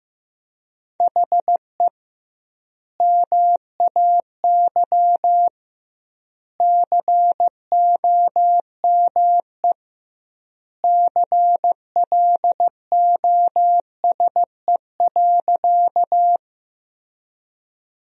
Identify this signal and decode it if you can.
Morse code, heard as a keyed tone, about 15 wpm, 700 Hz. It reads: HE MAY COME CLOSE.
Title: 2x Word-Spacing - Sentences from Top 300 Words - 15wpm